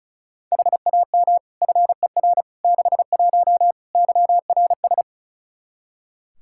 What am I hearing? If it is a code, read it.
Morse code, heard as a keyed tone, about 36 wpm, 700 Hz. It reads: HAM FER 61 YRS
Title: Practice QSO - Exchange 3 - More Info - 35wpm